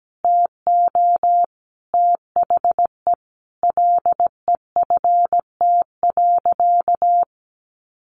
Morse code, heard as a keyed tone, about 17 wpm, 700 Hz. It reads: TO THE LEFT.
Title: Sentences from Top 300 Words 17wpm